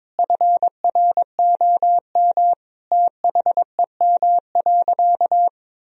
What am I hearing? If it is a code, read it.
Morse code, heard as a keyed tone, about 21 wpm, 700 Hz. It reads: FROM THEM.
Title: Sentences from Top 1000 Words - Review Part 2 - 22wpm